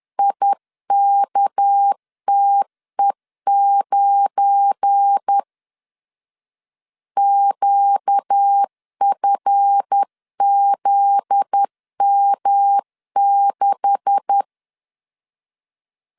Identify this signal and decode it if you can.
Morse code, heard as a keyed tone, about 10 wpm, 800 Hz. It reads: IKTE9 QFZM6
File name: LEZIONI / 9 286 10wpm 2735